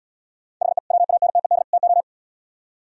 Morse code, heard as a keyed tone, about 76 wpm, 700 Hz. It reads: HE WANTED TO